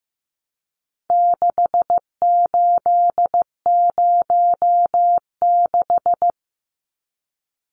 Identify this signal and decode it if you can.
Morse code, heard as a keyed tone, about 15 wpm, 700 Hz. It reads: 6806
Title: Numbers - 4-Digit 15wpm